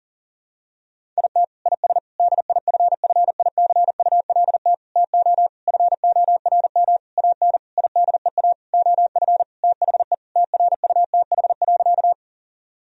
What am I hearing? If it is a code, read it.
Morse code, heard as a keyed tone, about 40 wpm, 700 Hz. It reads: IT IS DIFFICULT TO FORM AN IDEA OF THE TRUTH.
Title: Sentences from Top 1000 Words - Review Part 1 - 40wpm